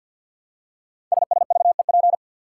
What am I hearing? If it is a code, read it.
Morse code, heard as a keyed tone, about 50 wpm, 700 Hz. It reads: SSVEP
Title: Code Groups - 5 Letters - 50wpm